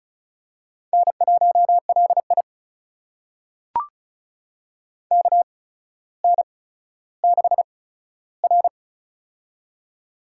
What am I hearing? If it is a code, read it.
Morse code, heard as a keyed tone, about 36 wpm, spaced as for 15 wpm, 700 Hz. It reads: N1LI KN6R
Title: USA Call Signs - 4 Characters - 8x Character Spacing - 35wpm